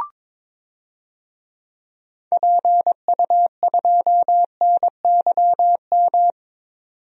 Morse code, heard as a keyed tone, about 22 wpm, 700 Hz. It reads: PU2NYM
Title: DXCC-100 Call Signs - 6 Characters - 22wpm